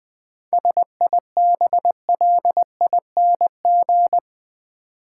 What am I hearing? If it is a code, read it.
Morse code, heard as a keyed tone, about 20 wpm, 700 Hz. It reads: SIBLING
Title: 7 Letter Words - First 3 Letters Given - 20wpm